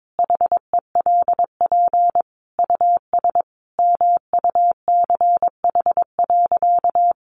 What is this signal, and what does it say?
Morse code, heard as a keyed tone, about 22 wpm, 700 Hz. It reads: HELP US MUCH.
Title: Sentences from Top 1500 Words 22wpm